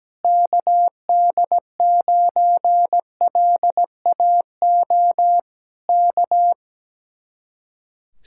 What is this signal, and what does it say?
Morse code, heard as a keyed tone, about 17 wpm, 700 Hz. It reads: KD9LAO K
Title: Speed-Racing Common QSO Phrases - 17wpm